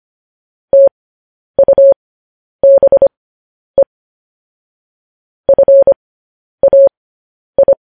Morse code, heard as a keyed tone, about 25 wpm, spaced as for 10 wpm, 550 Hz. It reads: TUBE FAI
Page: news headlines in morse code at 10 wpm